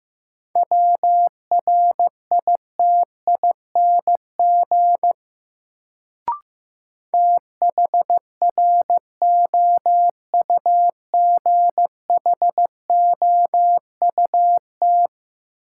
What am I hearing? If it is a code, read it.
Morse code, heard as a keyed tone, about 15 wpm, 700 Hz. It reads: WRITING THROUGHOUT
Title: Sentences from Top 800 Words - New Word Drill - 15wpm